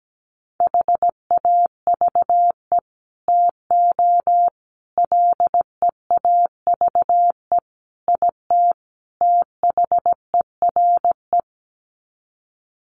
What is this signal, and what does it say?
Morse code, heard as a keyed tone, about 17 wpm, 700 Hz. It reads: HAVE TO LEAVE IT THERE
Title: Sets of 5 Words Encore2 17wpm